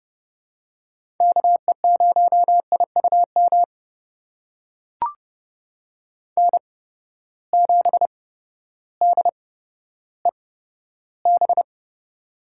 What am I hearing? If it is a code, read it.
Morse code, heard as a keyed tone, about 30 wpm, spaced as for 14 wpm, 700 Hz. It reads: KE0IUM N7DEB
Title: USA Call Signs - 8x Character Spacing - 30wpm